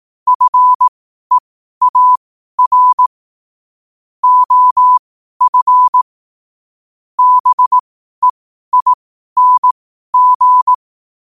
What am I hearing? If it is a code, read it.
Morse code, heard as a keyed tone, about 18 wpm, spaced as for 12 wpm, 1000 Hz. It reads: FEAR OF BEING
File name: qotdfiles / 12wpm